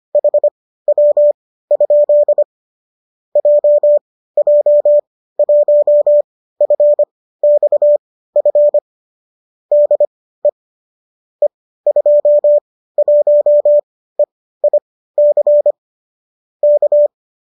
Morse code, heard as a keyed tone, about 25 wpm, spaced as for 15 wpm, 590 Hz. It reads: HW? JJ1FXF DE E21EIC K